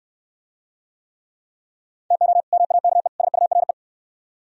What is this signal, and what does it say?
Morse code, heard as a keyed tone, about 66 wpm, 700 Hz. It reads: TO MAKE SURE